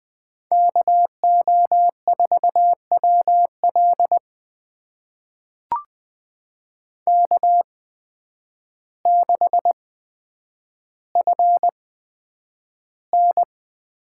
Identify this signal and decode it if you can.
Morse code, heard as a keyed tone, about 20 wpm, spaced as for 9 wpm, 700 Hz. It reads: KO4WL K6FN
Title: USA Call Signs - 5 Characters - 8x Character Spacing - 20wpm